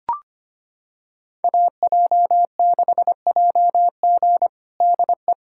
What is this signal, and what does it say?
Morse code, heard as a keyed tone, about 25 wpm, 700 Hz. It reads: AJ6JG DE
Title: USA Callsign DE Callsign 25wpm